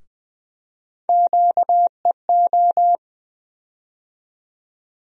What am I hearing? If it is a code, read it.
Morse code, heard as a keyed tone, about 20 wpm, 700 Hz. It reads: QEO
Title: USA Call Sign Postfixes - 3 Letters 20wpm